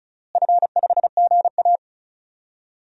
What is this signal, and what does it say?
Morse code, heard as a keyed tone, about 34 wpm, 700 Hz. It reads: F5GA
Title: DXCC-100 Call Signs - 4 Characters - Repeated 3x - 35wpm